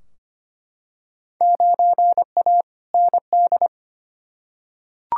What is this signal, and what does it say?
Morse code, heard as a keyed tone, about 25 wpm, 700 Hz. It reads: 9A ND
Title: Field Day Exchanges Encore - 25wpm